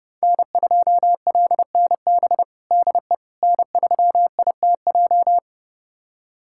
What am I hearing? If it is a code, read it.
Morse code, heard as a keyed tone, about 30 wpm, 700 Hz. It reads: N2LNB DE N3ITJ
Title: USA Callsign DE Callsign - 5 Characters - 30wpm